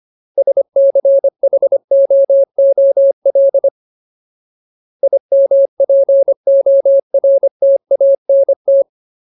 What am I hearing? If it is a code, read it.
Morse code, heard as a keyed tone, about 25 wpm, 550 Hz. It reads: SCHOOL IMPORTANT
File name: Ultimate Word List / 25WPM Ultimate